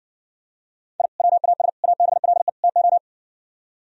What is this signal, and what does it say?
Morse code, heard as a keyed tone, about 60 wpm, 700 Hz. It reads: I WAS ABLE TO